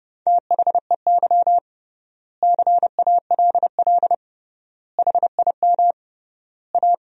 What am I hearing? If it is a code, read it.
Morse code, heard as a keyed tone, about 32 wpm, 700 Hz. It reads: THEY CALL HIM A
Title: 3x Word-Spacing - Sentences from Top 500 Words - 30wpm